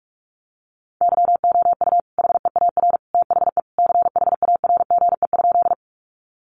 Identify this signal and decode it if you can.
Morse code, heard as a keyed tone, about 44 wpm, 700 Hz. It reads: YOU HEAR THE CHARGE?
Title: Sentences from Top 900 Words 45wpm